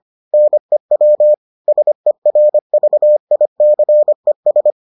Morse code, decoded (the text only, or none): NEW SERVICES